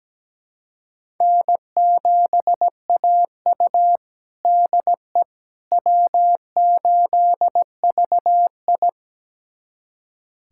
Morse code, heard as a keyed tone, about 17 wpm, 700 Hz. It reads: N7AU DE W8VI
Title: USA Callsign DE Callsign - 4 Characters - 17wpm